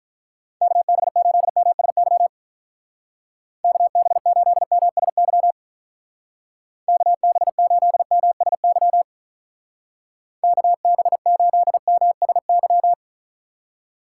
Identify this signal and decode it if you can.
Morse code, heard as a keyed tone, about 44 wpm, 700 Hz. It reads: KB8MSY KB8MSY KB8MSY KB8MSY